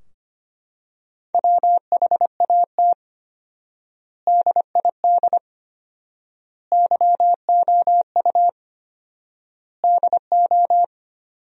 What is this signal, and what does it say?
Morse code, heard as a keyed tone, about 25 wpm, 700 Hz. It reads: WHAT DID YOU DO
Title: 4x Word-Spacing - Sentences from Top 100 Words - 25wpm